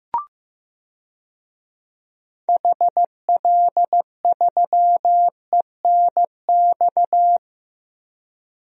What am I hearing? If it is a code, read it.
Morse code, heard as a keyed tone, about 15 wpm, 700 Hz. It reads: HL3ENX
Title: DXCC-100 Call Signs - 6 Characters - 15wpm